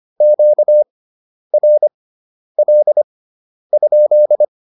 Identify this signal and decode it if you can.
Morse code, heard as a keyed tone, about 25 wpm, spaced as for 10 wpm, 600 Hz. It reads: QRL?
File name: QSO104 10